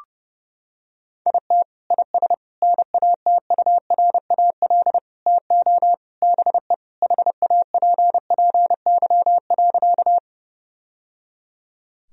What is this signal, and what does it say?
Morse code, heard as a keyed tone, about 30 wpm, 700 Hz. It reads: IT IS NATURAL TO BE HAPPY.
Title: Sentences from Top 1000 Words - Review Part 4 - 30wpm